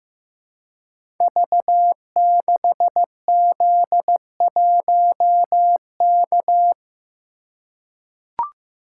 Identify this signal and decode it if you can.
Morse code, heard as a keyed tone, about 15 wpm, 700 Hz. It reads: V6Z1K
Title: Code Groups - 5 Letter:Numbers - 15wpm